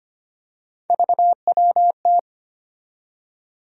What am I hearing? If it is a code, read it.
Morse code, heard as a keyed tone, about 25 wpm, 700 Hz. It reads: VWT
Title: USA Call Sign Postfixes - 3 Letters 25wpm